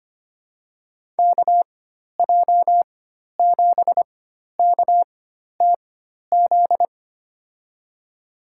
Morse code, heard as a keyed tone, about 25 wpm, spaced as for 12 wpm, 700 Hz. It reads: KJ7KTZ
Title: USA Call Signs - 6 Characters - 4x Character Spacing - 25wpm